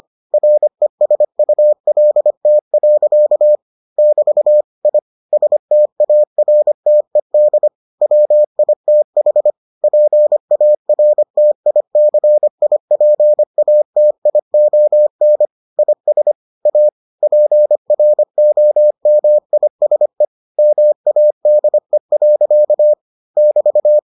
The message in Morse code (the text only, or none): RESULT. <BT> I STARTED WITH PARTICIPATION IS A PROMISE MADE. <BT>